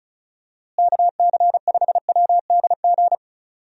Morse code, heard as a keyed tone, about 34 wpm, 700 Hz. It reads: KC5WDG
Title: USA Call Signs - 6 Characters - Repeated 3x - 35wpm